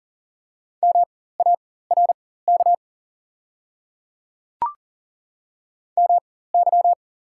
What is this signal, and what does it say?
Morse code, heard as a keyed tone, about 37 wpm, spaced as for 18 wpm, 700 Hz. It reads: MARK MY
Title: 4x Character Spacing - Top 500 Words - 40wpm